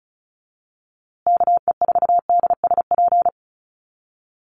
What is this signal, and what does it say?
Morse code, heard as a keyed tone, about 34 wpm, 700 Hz. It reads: KE4DSP
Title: USA Call Signs Repeated 3x 35wpm